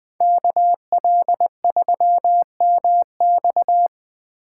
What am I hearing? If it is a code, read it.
Morse code, heard as a keyed tone, about 20 wpm, 700 Hz. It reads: KL3MX
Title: Speed Racing - USA 2x2 Call Signs 17wpm